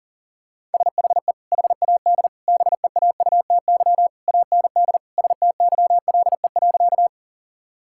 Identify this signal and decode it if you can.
Morse code, heard as a keyed tone, about 40 wpm, 690 Hz. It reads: SHE HAD BEAUTY AND STYLE.
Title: Sentences from Top 1000 Words - Review Part 5 - 40wpm